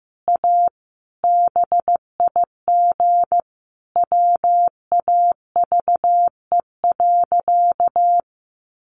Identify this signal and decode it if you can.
Morse code, heard as a keyed tone, about 15 wpm, 700 Hz. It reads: A BIG WAVE.
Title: Sentences from Top 1400 Words 15wpm